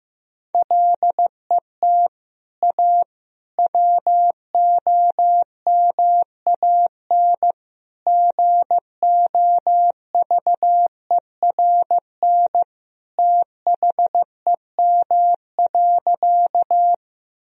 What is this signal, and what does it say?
Morse code, heard as a keyed tone, about 15 wpm, 700 Hz. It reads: LET A WOMAN GOVERN THEM.
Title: Sentences from Top 1000 Words - Review Part 2 - 15wpm